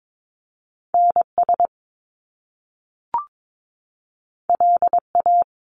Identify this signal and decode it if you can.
Morse code, heard as a keyed tone, about 22 wpm, 700 Hz. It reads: NS LA
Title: Top Two Letter Combinations in Words 22wpm